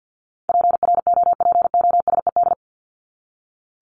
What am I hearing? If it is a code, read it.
Morse code, heard as a keyed tone, about 50 wpm, 700 Hz. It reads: PROPOSED
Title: ED Suffix - Top 100 Words with Base Words 50wpm